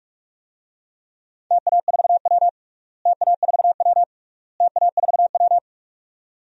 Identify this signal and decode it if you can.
Morse code, heard as a keyed tone, about 46 wpm, 690 Hz. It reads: TA4W TA4W TA4W
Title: DXCC-100 Call Signs - 4 Characters - Repeated 3x - 45wpm